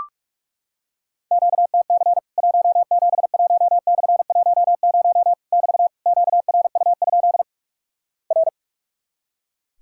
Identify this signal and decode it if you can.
Morse code, heard as a keyed tone, about 44 wpm, 700 Hz. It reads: QTC 171/10 <BT> QRU? R
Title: Worked All Europe DX Contest - QTC Exchange - 45wpm